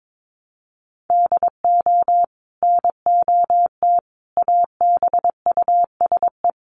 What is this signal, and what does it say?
Morse code, heard as a keyed tone, about 22 wpm, 700 Hz. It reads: DO NOT ABUSE
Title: Sentences from Top 1300 Words 22wpm